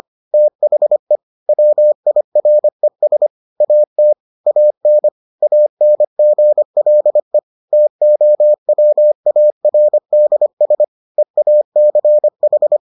Morse code, heard as a keyed tone, about 25 wpm, 600 Hz. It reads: THE WIRES AT AN ANGLE TOWARDS EACH